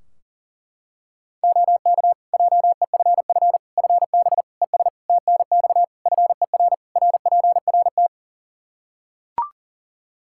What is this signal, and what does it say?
Morse code, heard as a keyed tone, about 41 wpm, 700 Hz. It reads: OK JEFF FB ES TNX FER RPRT
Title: K1USN SST QSO Practice 40wpm